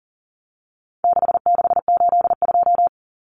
Morse code, heard as a keyed tone, about 40 wpm, 700 Hz. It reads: 6682
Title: Numbers - 4-Digit 40wpm